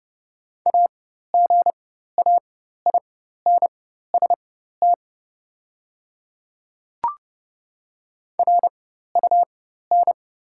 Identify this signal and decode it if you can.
Morse code, heard as a keyed tone, about 30 wpm, spaced as for 14 wpm, 690 Hz. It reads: AGAINST RUN
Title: 4x Character Spacing - Top 500 Words - 30wpm